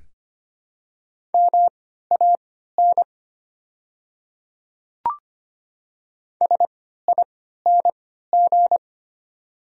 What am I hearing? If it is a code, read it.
Morse code, heard as a keyed tone, about 25 wpm, spaced as for 14 wpm, 690 Hz. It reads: MAN SING
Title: 3x Character Spacing - Top 500 Words - 25wpm